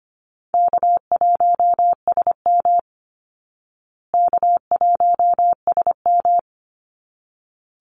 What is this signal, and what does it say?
Morse code, heard as a keyed tone, about 25 wpm, 700 Hz. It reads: K1SM K1SM
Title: Easy Call Signs Repeated 2x - 25wpm